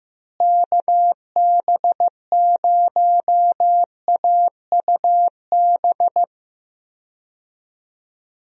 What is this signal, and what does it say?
Morse code, heard as a keyed tone, about 16 wpm, 700 Hz. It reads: KB0AUB
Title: USA Call Signs - 6 Characters - Repeated 3x - 15wpm